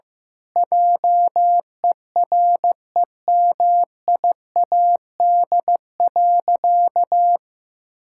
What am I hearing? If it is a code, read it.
Morse code, heard as a keyed tone, about 15 wpm, 700 Hz. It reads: JEREMIAD.
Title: Pangram Challenge v2 - 15wpm